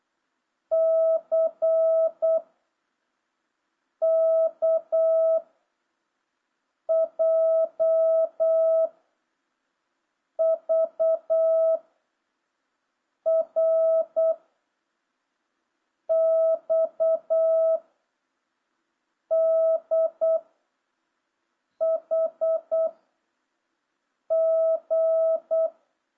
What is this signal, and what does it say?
Morse code, heard as a keyed tone, about 8 wpm, 660 Hz. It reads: C K J V R X D H G